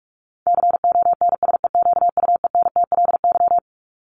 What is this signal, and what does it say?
Morse code, heard as a keyed tone, about 44 wpm, 700 Hz. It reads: CONSEQUENTLY
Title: LY Suffix - Top 100 Words with Base Words 45wpm